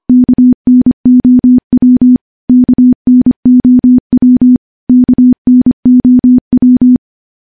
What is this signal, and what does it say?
Morse code, heard as a keyed tone, about 25 wpm, 250 Hz. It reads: KNOW KNOW KNOW
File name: morse / morse-training-25WPM